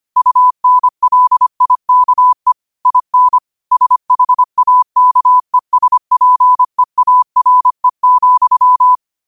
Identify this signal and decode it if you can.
Morse code, heard as a keyed tone, about 25 wpm, 1000 Hz. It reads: UNLIKE IN SHAKESPEARE,
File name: qotdfiles / 25wpm